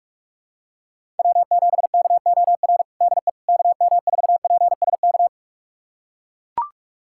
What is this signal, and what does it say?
Morse code, heard as a keyed tone, about 42 wpm, 700 Hz. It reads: W7KOR DE KM4PSK
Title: USA Callsign DE Callsign 45wpm